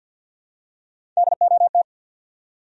Morse code, heard as a keyed tone, about 47 wpm, 700 Hz. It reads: DOT
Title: USA Common Names - 50wpm